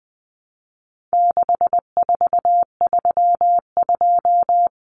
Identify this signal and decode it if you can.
Morse code, heard as a keyed tone, about 20 wpm, 700 Hz. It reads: 6432
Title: Common Cut Numbers - 4-Digit 20wpm